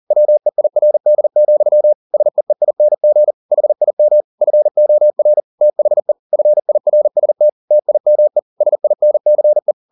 Morse code, heard as a keyed tone, about 40 wpm, 600 Hz. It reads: WEIRD, SEEING HIM FOR THE FIRST TIME SINCE